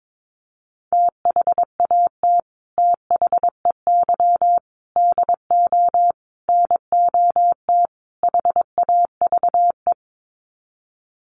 Morse code, heard as a keyed tone, about 22 wpm, 700 Hz. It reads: THAT THEY DO NOT HAVE
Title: Sets of 5 Words Encore2 22wpm